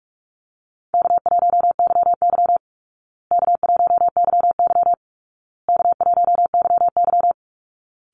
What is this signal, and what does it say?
Morse code, heard as a keyed tone, about 44 wpm, 700 Hz. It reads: K1YY K1YY K1YY
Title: Easy Call Signs Repeated 3x - 45wpm